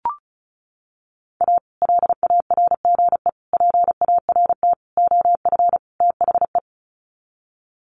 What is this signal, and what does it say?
Morse code, heard as a keyed tone, about 34 wpm, 700 Hz. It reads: A LARGE PART OF THE
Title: Sets of 5 Words Encore2 35wpm